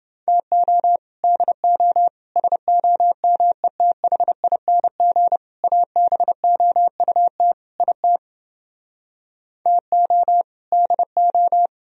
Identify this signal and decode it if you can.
Morse code, heard as a keyed tone, about 29 wpm, 700 Hz. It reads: TO DO SOMETHING ABOUT IT TO DO